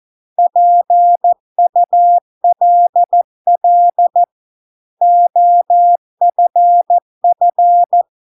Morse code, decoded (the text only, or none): PULL OFF